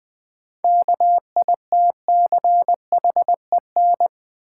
Morse code, heard as a keyed tone, about 20 wpm, 700 Hz. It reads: KITCHEN
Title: Sentences from Top 1400 Words - New Word Drill - 20wpm